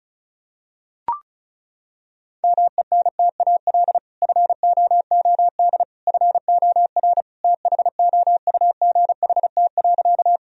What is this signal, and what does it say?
Morse code, heard as a keyed tone, about 36 wpm, 700 Hz. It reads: MENTAL FOOD FOR THOUGHT.
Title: Sentences from Top 1000 Words - Review Part 4 - 35wpm